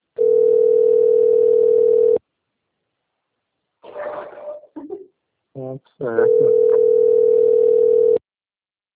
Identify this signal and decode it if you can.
Ringback tone (North America), 440 + 480 Hz, with speech.
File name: rg-1600-unknown- - - .29174